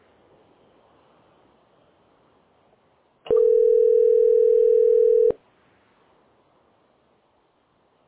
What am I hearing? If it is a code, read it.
Ringback tone (North America), 440 + 480 Hz.